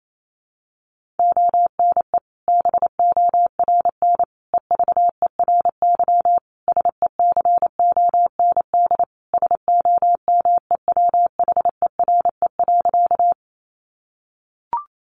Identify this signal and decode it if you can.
Morse code, heard as a keyed tone, about 28 wpm, 700 Hz. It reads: ONE BORN EVERY SECOND SOMEWHERE.
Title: Sentences from Top 1400 Words 28wpm